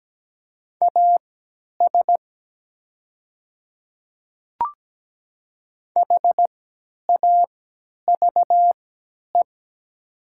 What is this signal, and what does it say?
Morse code, heard as a keyed tone, about 17 wpm, spaced as for 10 wpm, 700 Hz. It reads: AS HAVE
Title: 3x Character Spacing - Top 100 Words - 17wpm